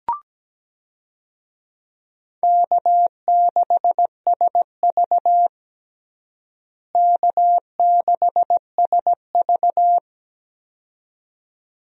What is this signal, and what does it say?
Morse code, heard as a keyed tone, about 17 wpm, 700 Hz. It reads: K6SV K6SV
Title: USA Call Signs - 4 Characters - Repeated 2x - 17wpm